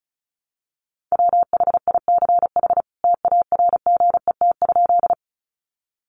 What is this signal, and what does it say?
Morse code, heard as a keyed tone, about 34 wpm, 700 Hz. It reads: WHICH TARGET?
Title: Sentences from Top 1200 Words 35wpm